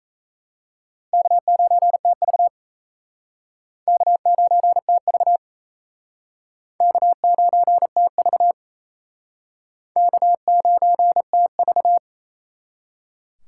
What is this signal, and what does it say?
Morse code, heard as a keyed tone, about 36 wpm, 700 Hz. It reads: K9TV K9TV K9TV K9TV